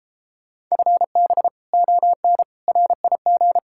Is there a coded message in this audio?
Yes: Morse code from 0.7 to 3.6 s, heard as a keyed tone, about 37 wpm, 700 Hz. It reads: FB ON RIG